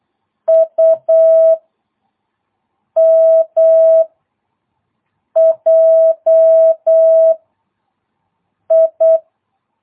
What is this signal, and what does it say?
Morse code, heard as a keyed tone, about 8 wpm, 660 Hz. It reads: U M J I